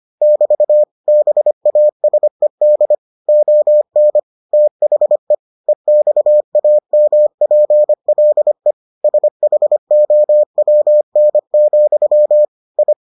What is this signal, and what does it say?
Morse code, heard as a keyed tone, about 25 wpm, 600 Hz. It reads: <BT> BASED ON THE EXAMPLE SHOWN, I